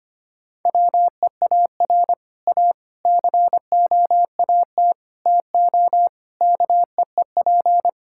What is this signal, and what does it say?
Morse code, heard as a keyed tone, about 25 wpm, 700 Hz. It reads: WEAR A COAT TO KEEP
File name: practice / 25wpm / Difficult Sentences with Hints - 25wpm